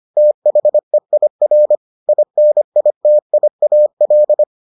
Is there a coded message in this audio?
Yes: Morse code from 0.2 to 4.4 s, heard as a keyed tone, about 25 wpm, 600 Hz. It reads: THEIR INITIAL